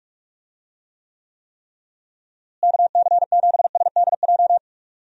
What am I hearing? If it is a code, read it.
Morse code, heard as a keyed tone, about 46 wpm, 700 Hz. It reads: KC7SDJ
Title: USA Call Signs - 6 Characters - Repeated 3x - 45wpm